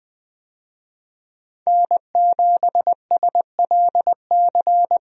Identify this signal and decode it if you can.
Morse code, heard as a keyed tone, about 20 wpm, 700 Hz. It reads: N7SLC
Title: USA Call Signs - 5 Characters - 20wpm